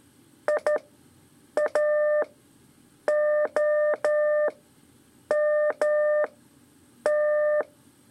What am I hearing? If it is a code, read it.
Morse code, heard as a keyed tone, about 12 wpm, spaced as for 7 wpm, 600 Hz. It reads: IAOMT
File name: 3 lezione CW 2 AIR